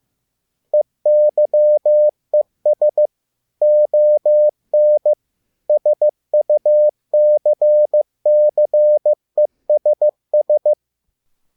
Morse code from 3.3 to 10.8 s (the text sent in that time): ON SUCCESS